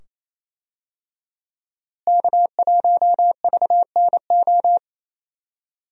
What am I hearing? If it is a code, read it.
Morse code, heard as a keyed tone, about 28 wpm, 700 Hz. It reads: K1VNO
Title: USA Call Signs - 5 Characters - Repeated 2x - 28wpm